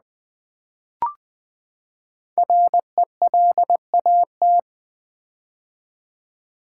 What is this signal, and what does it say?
Morse code, heard as a keyed tone, about 20 wpm, 700 Hz. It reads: RELAT
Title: Build A Word - 5 Letters At A Time - 12-Letter Words - 20wpm